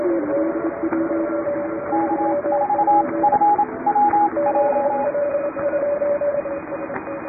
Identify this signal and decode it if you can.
Morse code, heard as a keyed tone, about 28 wpm, spaced as for 22 wpm, 830 Hz, with radio noise. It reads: K4FWJ
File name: chatham / 2350-12CW